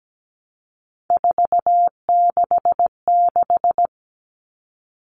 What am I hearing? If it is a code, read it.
Morse code, heard as a keyed tone, about 17 wpm, 700 Hz. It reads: 466
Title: Numbers - 3-Digit 17wpm